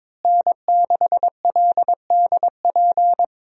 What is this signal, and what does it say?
Morse code, heard as a keyed tone, about 22 wpm, 700 Hz. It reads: N6LDP
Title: USA Call Signs - 5 Characters - Repeated 2x - 22wpm